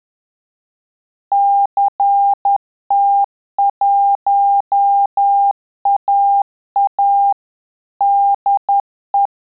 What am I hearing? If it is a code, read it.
Morse code, heard as a keyed tone, about 10 wpm, 800 Hz. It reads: CT1AA DE
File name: Telegraphy-group / QSO CW 10 PPM